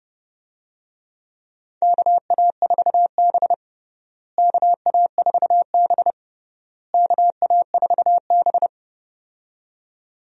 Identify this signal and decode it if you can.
Morse code, heard as a keyed tone, about 32 wpm, 700 Hz. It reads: KA4B KA4B KA4B
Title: USA Call Signs - 4 Characters - Repeated 3x - 30wpm